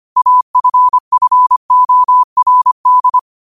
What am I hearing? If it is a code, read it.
Morse code, heard as a keyed tone, about 25 wpm, 1000 Hz. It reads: AFFORD